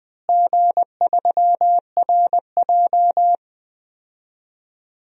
Morse code, heard as a keyed tone, about 20 wpm, 700 Hz. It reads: G3RJ
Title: DXCC-100 Call Signs - 4 Characters - Repeated 2x - 20wpm